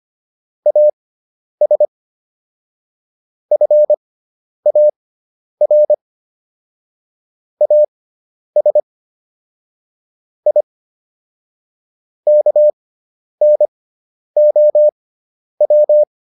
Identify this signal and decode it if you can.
Morse code, heard as a keyed tone, about 25 wpm, spaced as for 10 wpm, 600 Hz. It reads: AS FAR AS I KNOW